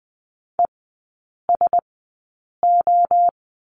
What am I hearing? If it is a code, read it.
Morse code, heard as a keyed tone, about 20 wpm, spaced as for 8 wpm, 700 Hz. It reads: ESO
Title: 035 Lesson 7e - Letter s - Three Characters 20wpm